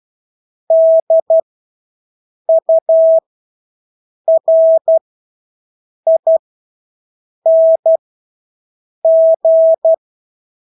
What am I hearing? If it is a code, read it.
Morse code, heard as a keyed tone, about 12 wpm, spaced as for 6 wpm, 650 Hz. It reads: DURING
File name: Engelske ord 12-6 wpm